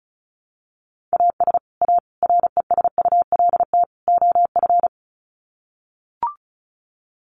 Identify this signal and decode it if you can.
Morse code, heard as a keyed tone, about 34 wpm, 700 Hz. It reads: AS A RESULT OF
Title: Sets of 4 Words 35wpm